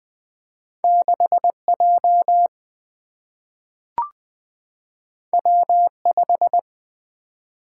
Morse code, heard as a keyed tone, about 20 wpm, 700 Hz. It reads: 6J W5
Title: Single-Letter Single-Number 20wpm